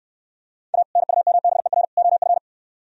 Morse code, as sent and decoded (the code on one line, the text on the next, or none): .- / -. ..- -- -... .-. / --- ..-.
A NUMBER OF